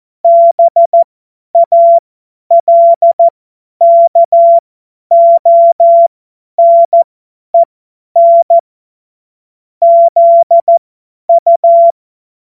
Morse code, heard as a keyed tone, about 14 wpm, spaced as for 10 wpm, 680 Hz. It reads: BALKONEN ZU
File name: Morsekurs / Witze / Witz 22. Juni 2021, 50 F70